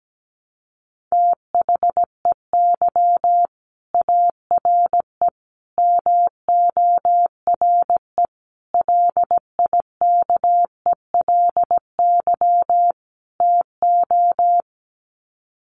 Morse code, heard as a keyed tone, about 17 wpm, 700 Hz. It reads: THEY ARE MORE LIKELY TO
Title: Sets of 5 Words Encore2 17wpm